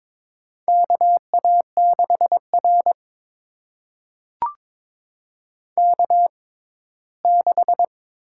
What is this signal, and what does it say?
Morse code, heard as a keyed tone, about 22 wpm, spaced as for 15 wpm, 700 Hz. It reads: KA6R K6
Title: USA Call Signs - 4 Characters - 6x Character Spacing - 22wpm